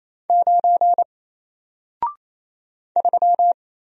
Morse code, heard as a keyed tone, about 27 wpm, 700 Hz. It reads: 9 3
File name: Single-Letter-Number-28wpm